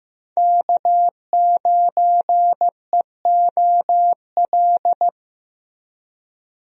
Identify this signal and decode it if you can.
Morse code, heard as a keyed tone, about 15 wpm, 700 Hz. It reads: K9EOL
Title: USA Call Signs - 5 Characters - Repeated 2x - 15wpm